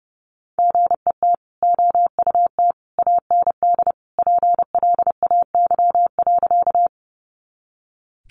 Morse code, heard as a keyed tone, about 30 wpm, 700 Hz. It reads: GET OUT AND PLAY.
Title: Sentences from Top 200 Words 30wpm